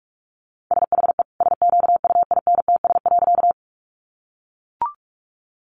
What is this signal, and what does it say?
Morse code, heard as a keyed tone, about 44 wpm, 700 Hz. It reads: SHE SQUINTS.
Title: Additional Easy Phrases 45wpm